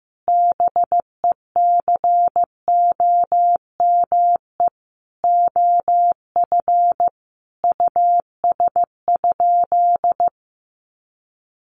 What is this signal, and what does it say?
Morse code, heard as a keyed tone, about 15 wpm, 700 Hz. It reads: BECOME OF US?
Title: Sentences from Top 600 Words 15wpm